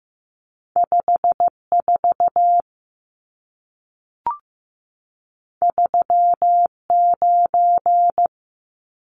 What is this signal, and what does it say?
Morse code, heard as a keyed tone, about 15 wpm, 700 Hz. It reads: 54 39
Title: Numbers - 2-Digit 15wpm